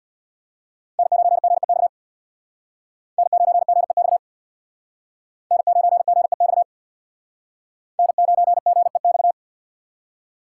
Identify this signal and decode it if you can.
Morse code, heard as a keyed tone, about 63 wpm, 700 Hz. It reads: N9GEX N9GEX N9GEX N9GEX